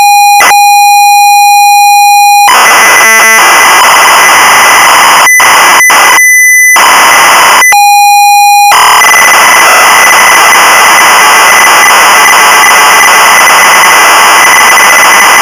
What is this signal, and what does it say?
ZX Spectrum tape loading.